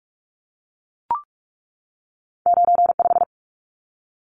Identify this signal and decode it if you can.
Morse code, heard as a keyed tone, about 44 wpm, 700 Hz. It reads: 95
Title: Numbers - 2-Digit 45wpm